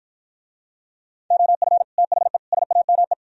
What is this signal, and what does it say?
Morse code, heard as a keyed tone, about 52 wpm, 700 Hz. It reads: OF THE SAME